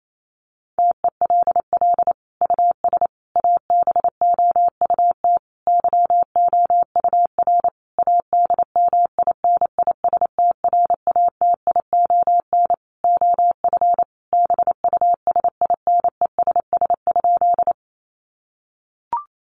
Morse code, heard as a keyed tone, about 28 wpm, 700 Hz. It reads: TELL US ABOUT YOUR ADMINISTRATION OF BUSINESS?
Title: Sentences from Top 1300 Words 28wpm